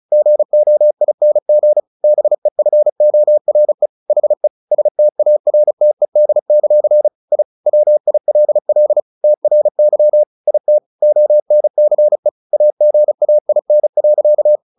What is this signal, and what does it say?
Morse code, heard as a keyed tone, about 34 wpm, 600 Hz. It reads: GOING BEFORE HE STARTED; I WILL TRY IT ONCE AGAIN.
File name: the old man and the sea 2 35wpm